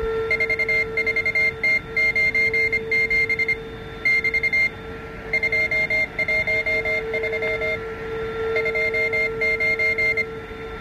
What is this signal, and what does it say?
Morse code, heard as a keyed tone, about 25 wpm, 2100 Hz. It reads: 44T97 <BT> 213 29